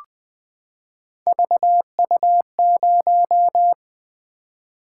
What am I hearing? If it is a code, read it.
Morse code, heard as a keyed tone, about 20 wpm, 700 Hz. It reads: VU0
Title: Three Letter-Number Combinations - 20wpm